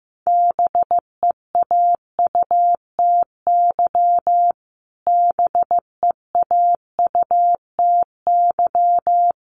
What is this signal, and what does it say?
Morse code, heard as a keyed tone, about 15 wpm, 700 Hz. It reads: BEAUTY BEAUTY
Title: Top 500 Words Repeated 15wpm